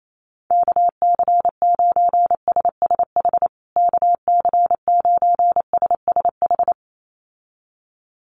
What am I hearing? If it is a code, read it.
Morse code, heard as a keyed tone, about 28 wpm, 700 Hz. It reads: KC9SSH KC9SSH
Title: USA Call Signs Repeated 2x 28wpm